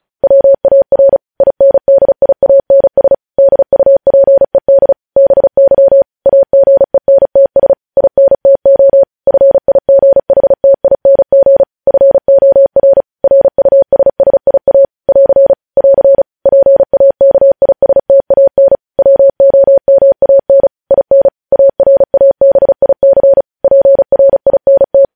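Morse code, heard as a keyed tone, about 36 wpm, 550 Hz. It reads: WAR INDIANS DUPED BY AGENTS INTO FIGHTING FOR RUSSIA <AR> <AR> PAKISTAN WOMAN IN ARABIC PRINT
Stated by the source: Morse practice sending news headlines at 35 wpm.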